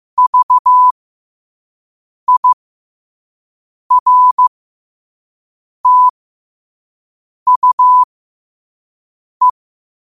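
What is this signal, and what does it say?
Morse code, heard as a keyed tone, about 15 wpm, spaced as for 5 wpm, 1000 Hz. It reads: VIRTUE
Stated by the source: Morse code practice at 5 wpm.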